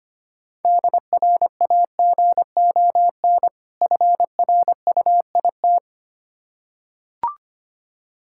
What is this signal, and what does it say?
Morse code, heard as a keyed tone, about 25 wpm, 700 Hz. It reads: DRAGON FRUIT
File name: practice / 25wpm / Difficult Words with Hints - 25wpm